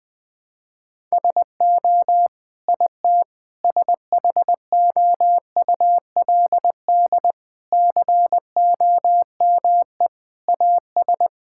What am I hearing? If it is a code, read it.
Morse code, heard as a keyed tone, about 20 wpm, 700 Hz. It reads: SO IT SHOULD COME AS
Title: Sentences with 9-Word Phrases - 20wpm